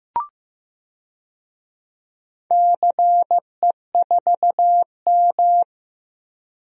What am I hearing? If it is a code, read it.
Morse code, heard as a keyed tone, about 16 wpm, 700 Hz. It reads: CE4M
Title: DXCC-100 Call Signs - 4 Characters - 15wpm